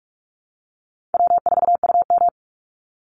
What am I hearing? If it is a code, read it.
Morse code, heard as a keyed tone, about 44 wpm, 700 Hz. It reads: W4UM
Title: Easy Call Signs Repeated 3x - 45wpm